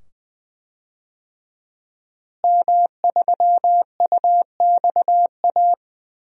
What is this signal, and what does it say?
Morse code, heard as a keyed tone, about 21 wpm, 700 Hz. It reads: M3UXA
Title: DXCC-100 Call Signs - 5 Characters - Repeated 2x - 20wpm